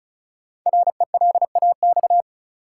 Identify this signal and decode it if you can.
Morse code, heard as a keyed tone, about 36 wpm, 700 Hz. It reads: RELAX
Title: Sentences from Top 1400 Words - New Word Drill Alt - 35wpm